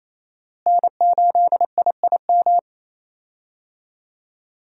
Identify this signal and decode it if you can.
Morse code, heard as a keyed tone, about 27 wpm, 700 Hz. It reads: N8IIM
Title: USA Call Signs - 5 Characters - Repeated 2x - 28wpm